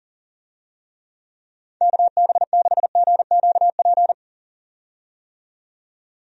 Morse code, heard as a keyed tone, about 40 wpm, 700 Hz. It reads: KB6GQP
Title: USA Call Signs - 6 Characters - 40wpm